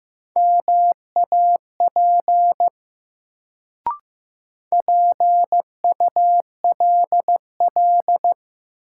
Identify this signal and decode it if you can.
Morse code, heard as a keyed tone, about 15 wpm, 700 Hz. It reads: MAP PULL
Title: Sentences from Top 400 Words - New Word Drill - 15wpm